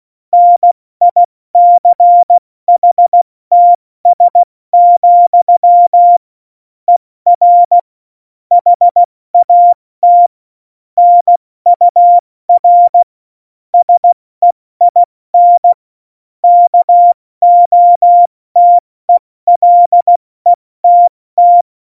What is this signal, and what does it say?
Morse code, heard as a keyed tone, about 16 wpm, 700 Hz. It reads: NICHTS, ER HAT NUR SEIN KOTELETT
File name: Morsekurs / Witze / Witz 10. September 2019, 70 F80